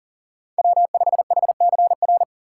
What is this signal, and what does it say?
Morse code, heard as a keyed tone, about 40 wpm, 700 Hz. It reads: W5HCR
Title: USA Call Signs - 5 Characters - 4x Character Spacing - 40wpm